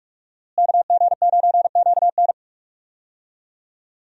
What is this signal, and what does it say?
Morse code, heard as a keyed tone, about 41 wpm, 700 Hz. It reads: KG9QN
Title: USA Call Signs - 5 Characters - Repeated 2x - 45wpm